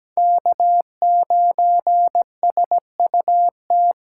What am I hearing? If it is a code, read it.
Morse code, heard as a keyed tone, about 20 wpm, 700 Hz. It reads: K9SUT
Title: Speed Racing - USA 1x3 Call Signs 17wpm